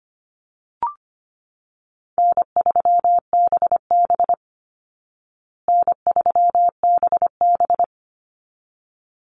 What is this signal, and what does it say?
Morse code, heard as a keyed tone, about 25 wpm, 700 Hz. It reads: N3BB N3BB
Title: Call Signs Repeated 3x - 25wpm